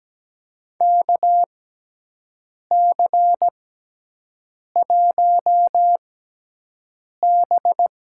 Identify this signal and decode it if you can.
Morse code, heard as a keyed tone, about 17 wpm, spaced as for 6 wpm, 700 Hz. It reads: KC1B
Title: USA Call Signs - 6x Character Spacing - 17wpm